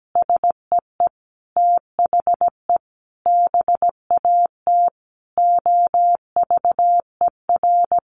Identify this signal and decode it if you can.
Morse code, heard as a keyed tone, about 17 wpm, 700 Hz. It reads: SEE THE BAT OVER
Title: Sentences from Top 1500 Words 17wpm